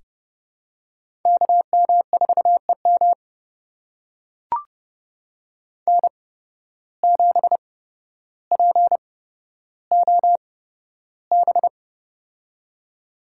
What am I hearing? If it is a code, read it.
Morse code, heard as a keyed tone, about 30 wpm, spaced as for 13 wpm, 700 Hz. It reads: KM4EM N7POB
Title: USA Call Signs - 5 Characters - 8x Character Spacing - 30wpm